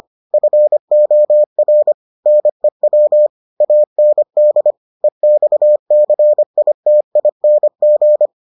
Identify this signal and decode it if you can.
Morse code, heard as a keyed tone, about 25 wpm, 600 Hz. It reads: FOR NEW AND EXCITING